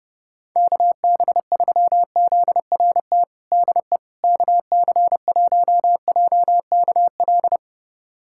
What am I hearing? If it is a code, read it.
Morse code, heard as a keyed tone, about 32 wpm, 700 Hz. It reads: KB3ZRT DE KC1JKL
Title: USA Callsign DE Callsign 30wpm